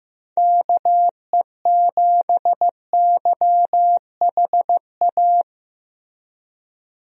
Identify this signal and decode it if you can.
Morse code, heard as a keyed tone, about 15 wpm, 700 Hz. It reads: KE7YHA
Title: USA Call Signs - 6 Characters - Repeated 3x - 15wpm